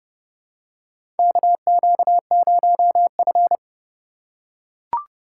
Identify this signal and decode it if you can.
Morse code, heard as a keyed tone, about 30 wpm, 700 Hz. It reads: KQ0F
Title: USA Call Signs - 4 Characters - 3x Character Spacing - 30wpm